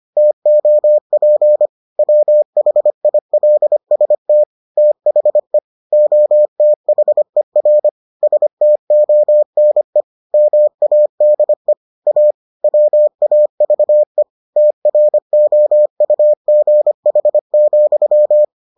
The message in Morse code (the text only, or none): TOP WHILST THE OTHER STONE MADE A WAVE TROUGH,